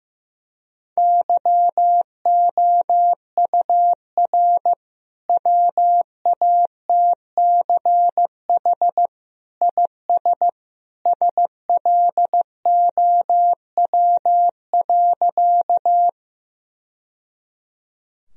Morse code, heard as a keyed tone, about 15 wpm, 700 Hz. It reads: YOUR WATCH IS SLOW.
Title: Sentences from Top 1000 Words - Review Part 2 - 15wpm